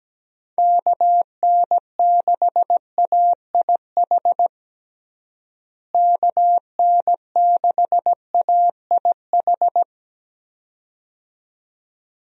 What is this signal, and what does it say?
Morse code, heard as a keyed tone, about 17 wpm, 700 Hz. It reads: KN6AIH KN6AIH
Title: USA Call Signs - 6 Characters - Repeated 2x - 17wpm